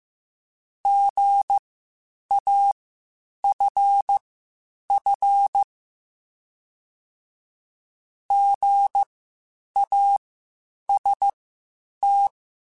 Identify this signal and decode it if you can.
Morse code, heard as a keyed tone, about 15 wpm, spaced as for 8 wpm, 800 Hz. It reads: GAFF GAST
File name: Spaced Alphas / Spaced Alphas mp3s / Gs sp @13